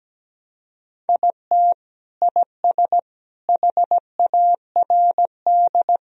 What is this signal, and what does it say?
Morse code, heard as a keyed tone, about 17 wpm, 700 Hz. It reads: IT IS HARD
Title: Sentences from Top 1000 Words - Review Part 1 - 17wpm